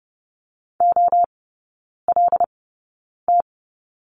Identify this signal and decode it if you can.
Morse code, heard as a keyed tone, about 30 wpm, spaced as for 9 wpm, 700 Hz. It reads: OLT
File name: Lesson 16e - Letter c - Three Characters 30wpm